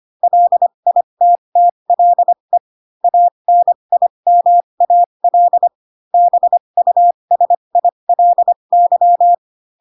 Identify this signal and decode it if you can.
Morse code, heard as a keyed tone, about 25 wpm, spaced as for 22 wpm, 700 Hz. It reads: LITTLE ANIMAL BUSILY
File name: Pinocchio / Pinocchio ch7 18 25 ik1hge